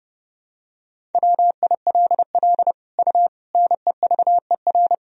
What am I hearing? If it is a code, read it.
Morse code, heard as a keyed tone, about 32 wpm, 700 Hz. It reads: WILL U NEVER
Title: Sentences from Top 300 Words and Abbreviations 30wpm